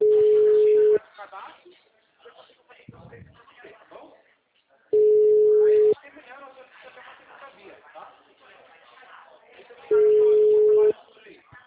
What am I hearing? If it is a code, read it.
Ringback tone (Europe, 425 Hz), with speech.